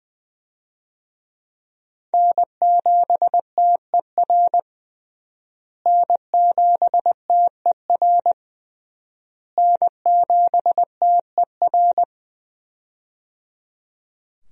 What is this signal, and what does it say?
Morse code, heard as a keyed tone, about 20 wpm, 700 Hz. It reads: N7TER N7TER N7TER
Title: USA Call Signs - 5 Characters - Repeated 3x - 20wpm